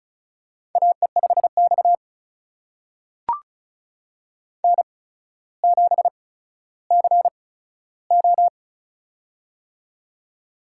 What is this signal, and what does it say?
Morse code, heard as a keyed tone, about 36 wpm, spaced as for 15 wpm, 700 Hz. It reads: AE5X N7CO
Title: USA Call Signs - 4 Characters - 8x Character Spacing - 35wpm